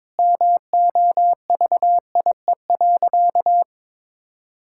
Morse code, heard as a keyed tone, about 22 wpm, 700 Hz. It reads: MOVIE.
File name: practice / 22wpm / Sentences with 4-Word Phrases - 22wpm